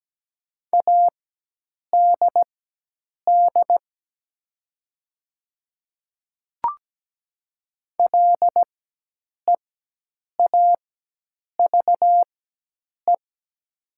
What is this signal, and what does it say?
Morse code, heard as a keyed tone, about 17 wpm, spaced as for 8 wpm, 700 Hz. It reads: ADD LEAVE
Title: 4x Character Spacing - Top 500 Words - 17wpm